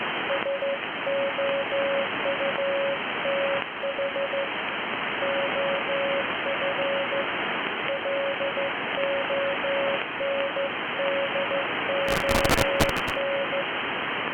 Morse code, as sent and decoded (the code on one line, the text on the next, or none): ... --- ..- - .... / --- ..-. / .-.. --- -. -.. --- -.
SOUTH OF LONDON